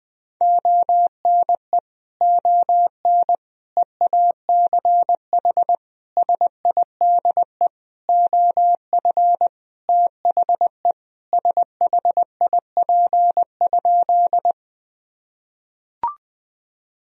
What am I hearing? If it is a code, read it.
Morse code, heard as a keyed tone, about 20 wpm, 700 Hz. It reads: ONE ON EACH SIDE OF THE SHIP?
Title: Sentences from Top 1000 Words - Review Part 2 - 20wpm